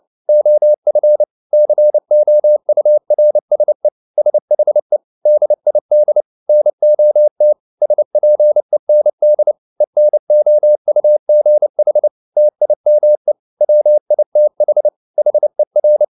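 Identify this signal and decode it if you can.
Morse code, heard as a keyed tone, about 29 wpm, 600 Hz. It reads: OF COURSE SHE DID NOT SPEND ENOUGH TIME WITH HER
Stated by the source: Morse code at 29 wpm, sending Ken Follett's novel Never in English, from CW med Gnister.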